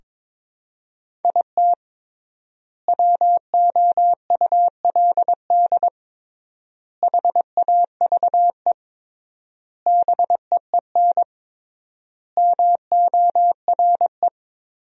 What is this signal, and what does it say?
Morse code, heard as a keyed tone, about 22 wpm, 700 Hz. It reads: IT WOULD HAVE BEEN MORE
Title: 3x Word-Spacing - Sentences from Top 400 Words - 22wpm